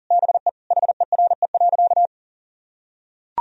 Morse code, heard as a keyed tone, about 40 wpm, 700 Hz. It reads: BE HERE.
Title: Sentences from Top 1000 Words - Review Part 4 - 40wpm